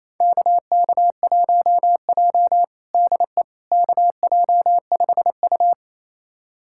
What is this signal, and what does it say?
Morse code, heard as a keyed tone, about 28 wpm, 700 Hz. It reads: KK1J DE KJ5U
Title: USA Callsign DE Callsign - 4 Characters - 28wpm